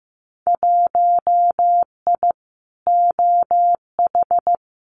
Morse code, heard as a keyed tone, about 15 wpm, 700 Hz. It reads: 1I OH
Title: Winter Field Day Exchanges 15wpm